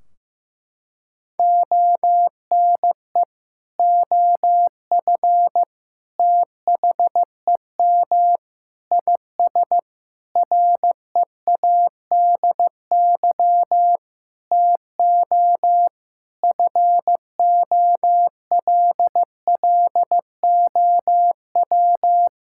Morse code, heard as a keyed tone, about 16 wpm, 700 Hz. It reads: ONE OF THEM IS READY TO FOLLOW
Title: Sentences from Top 1600 Words 15wpm